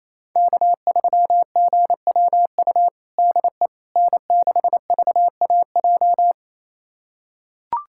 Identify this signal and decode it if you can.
Morse code, heard as a keyed tone, about 28 wpm, 700 Hz. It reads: K3GWU DE N6VAJ
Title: USA Callsign DE Callsign - 5 Characters - 28wpm